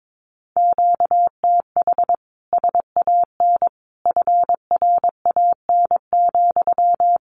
Morse code, heard as a keyed tone, about 22 wpm, 700 Hz. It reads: QTH SAN FRAN,
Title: Common QSO Phrases Encore 22wpm